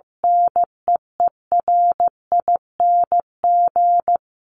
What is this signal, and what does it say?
Morse code, heard as a keyed tone, about 15 wpm, 700 Hz. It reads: NEERING
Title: Top 1251-1500 Words 15wpm